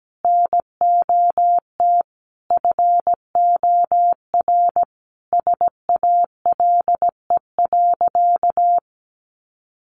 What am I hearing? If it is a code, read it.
Morse code, heard as a keyed tone, about 17 wpm, 700 Hz. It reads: NOT FOR SALE.
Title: Sentences from Top 1400 Words 17wpm